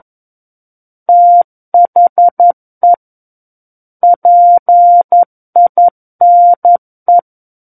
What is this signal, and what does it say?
Morse code, heard as a keyed tone, about 11 wpm, 700 Hz. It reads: THE PINE
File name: Book One / 02-Chapter 2 - 11 WPM